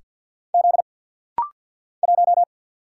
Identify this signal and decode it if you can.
Morse code, heard as a keyed tone, about 48 wpm, 700 Hz. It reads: Z 1
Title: Single Letter-Number - Rapid-Fire 50wpm